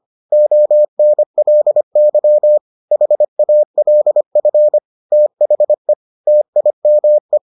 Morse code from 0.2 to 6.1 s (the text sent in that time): ONLY HALF THE